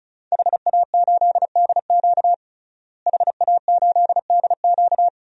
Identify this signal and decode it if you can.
Morse code, heard as a keyed tone, about 34 wpm, 700 Hz. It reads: HA8DQ HA8DQ
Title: DXCC-100 Call Signs - 5 Characters - Repeated 3x - 35wpm